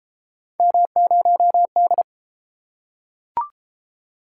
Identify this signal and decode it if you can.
Morse code, heard as a keyed tone, about 29 wpm, 700 Hz. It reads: M0D